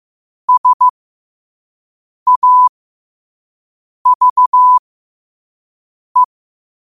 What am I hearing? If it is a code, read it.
Morse code, heard as a keyed tone, about 15 wpm, spaced as for 5 wpm, 1000 Hz. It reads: SAVE
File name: qotdfiles / 5wpm